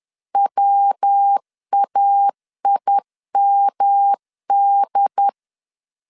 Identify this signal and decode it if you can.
Morse code, heard as a keyed tone, about 10 wpm, 800 Hz. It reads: WAIMD
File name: CW / DET / 1 3 1500C S B 10wpm 3335